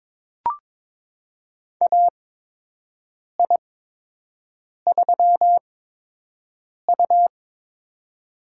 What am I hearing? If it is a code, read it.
Morse code, heard as a keyed tone, about 22 wpm, spaced as for 6 wpm, 700 Hz. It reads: AI3U
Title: USA Call Signs - 4 Characters - 8x Character Spacing - 22wpm